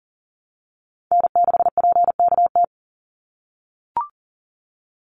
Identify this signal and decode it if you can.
Morse code, heard as a keyed tone, about 40 wpm, 700 Hz. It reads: N6PKT
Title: NA SOTA Chaser Callsigns 40wpm